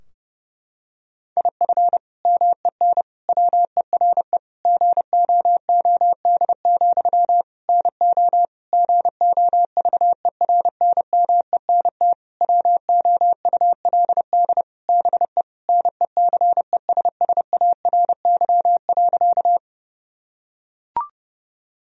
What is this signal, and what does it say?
Morse code, heard as a keyed tone, about 30 wpm, 700 Hz. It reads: IF MEN WERE GOOD, NO GOVERNMENT WOULD BE NECESSARY.
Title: Sentences from Top 1000 Words - Review Part 3 - 30wpm